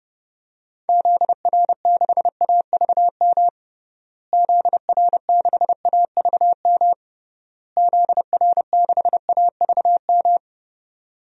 Morse code, heard as a keyed tone, about 32 wpm, 700 Hz. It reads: ZR6AVM ZR6AVM ZR6AVM
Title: DXCC-100 Call Signs - 6 Characters - Repeated 3x - 30wpm